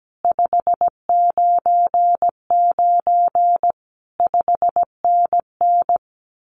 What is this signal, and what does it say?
Morse code, heard as a keyed tone, about 17 wpm, 700 Hz. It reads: 599 5NN
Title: Common QSO Phrases Encore 17wpm